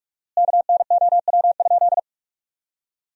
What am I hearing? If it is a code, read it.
Morse code, heard as a keyed tone, about 46 wpm, 700 Hz. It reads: KNOW?
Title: 4x Word-Spacing - Sentences from Top 200 Words - 45wpm